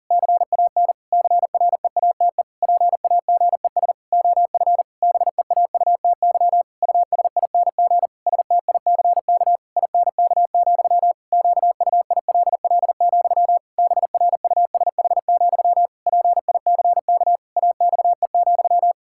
Morse code, heard as a keyed tone, about 40 wpm, 700 Hz. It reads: CAN CREATE PAGES OF BEAUTY USING STICK INK, QUILL, BRUSH, PICK AXE,
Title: Pangram Challenge v2 - 40wpm